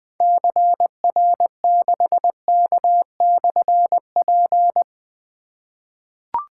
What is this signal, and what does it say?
Morse code, heard as a keyed tone, about 21 wpm, 700 Hz. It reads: CR6K/P
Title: Difficult Call Signs 20wpm